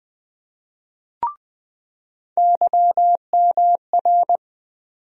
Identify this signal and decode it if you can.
Morse code, heard as a keyed tone, about 20 wpm, 700 Hz. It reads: YMR
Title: USA Call Sign Postfixes - 3 Letters 20wpm